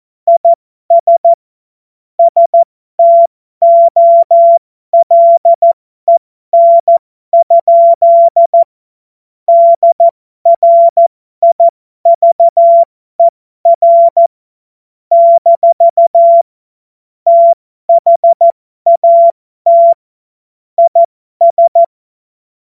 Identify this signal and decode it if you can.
Morse code, heard as a keyed tone, about 14 wpm, spaced as for 12 wpm, 680 Hz. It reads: IS STOLEN? DRIVER - THAT IS